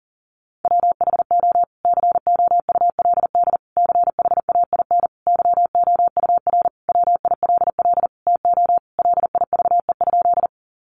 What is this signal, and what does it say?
Morse code, heard as a keyed tone, about 40 wpm, 700 Hz. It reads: WHO COULD CHAIN YOUR WILL TO LIVE?
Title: Sentences from Top 1500 Words 40wpm